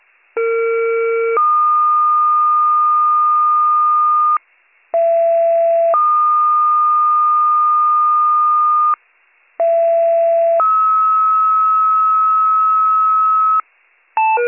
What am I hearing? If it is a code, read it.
Two-tone page, heard as a one-second tone, then a three-second tone, about 460 then 1220 Hz, about 670 then 1220 Hz, about 670 then 1290 Hz. It is heard from 0.4 to 13.6 s.